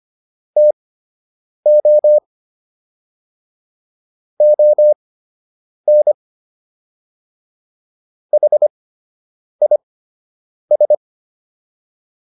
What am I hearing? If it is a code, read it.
Morse code, heard as a keyed tone, about 25 wpm, spaced as for 8 wpm, 600 Hz. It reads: TO ON HIS